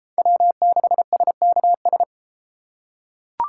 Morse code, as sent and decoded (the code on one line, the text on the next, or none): .-- -.... ... -.- ...
W6SKS